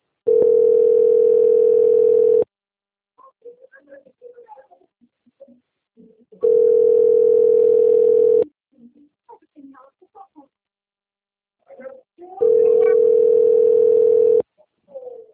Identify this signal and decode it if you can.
Ringback tone (North America), 440 + 480 Hz, with speech.
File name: out- - - - - .30932